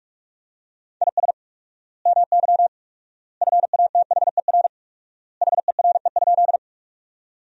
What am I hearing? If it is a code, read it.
Morse code, heard as a keyed tone, about 46 wpm, 700 Hz. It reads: IS MY FATHER HERE?
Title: 4x Word-Spacing - Sentences from Top 300 Words - 45wpm